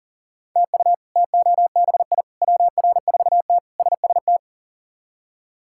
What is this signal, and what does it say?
Morse code, heard as a keyed tone, about 40 wpm, 700 Hz. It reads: TU TOBI WR4T SST
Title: K1USN SST QSO Practice - 40wpm